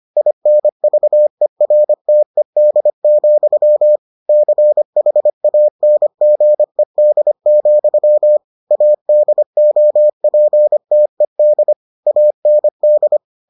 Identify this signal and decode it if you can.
Morse code, heard as a keyed tone, about 25 wpm, 600 Hz. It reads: INVERTED, CHANGED, ADOPTED AND